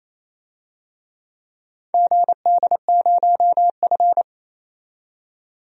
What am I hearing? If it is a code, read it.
Morse code, heard as a keyed tone, about 28 wpm, 700 Hz. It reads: GD0F
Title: DXCC-100 Call Signs - 4 Characters - 28wpm